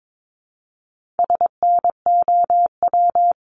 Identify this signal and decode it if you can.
Morse code, heard as a keyed tone, about 22 wpm, 700 Hz. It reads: SNOW
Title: Multi-letter Phonemes Repeated 3x 22wpm